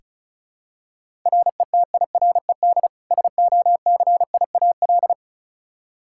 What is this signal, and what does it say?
Morse code, heard as a keyed tone, about 36 wpm, 700 Hz. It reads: RETIRED SOCIAL
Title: Practice QSO - Exchange 3 - More Info - 35wpm